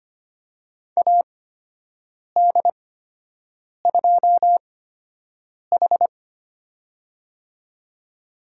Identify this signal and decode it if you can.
Morse code, heard as a keyed tone, about 25 wpm, spaced as for 7 wpm, 700 Hz. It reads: AD2H
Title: USA Call Signs - 4 Characters - 8x Character Spacing - 25wpm